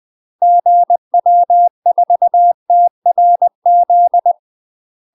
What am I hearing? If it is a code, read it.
Morse code, heard as a keyed tone, about 20 wpm, 700 Hz. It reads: GW4TRZ
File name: UK MASTER QSOs TRAINING / 20WPM / UK MASTER QSOs 0049